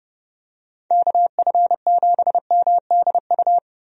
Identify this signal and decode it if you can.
Morse code, heard as a keyed tone, about 31 wpm, 700 Hz. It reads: KF7MDU
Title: USA Call Signs - 6 Characters - 30wpm